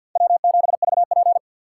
Morse code, heard as a keyed tone, about 51 wpm, 690 Hz. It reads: W7VP
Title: K1USN SST QSO Practice - 50wpm